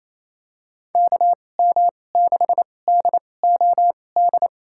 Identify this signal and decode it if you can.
Morse code, heard as a keyed tone, about 28 wpm, spaced as for 20 wpm, 700 Hz. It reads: KM6DOD
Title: USA Call Signs - 2x Character Spacing - 28wpm